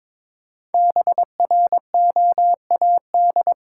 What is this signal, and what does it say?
Morse code, heard as a keyed tone, about 22 wpm, 700 Hz. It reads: BROAD
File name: 5 Letter Words - First 3 Letters Given - 22wpm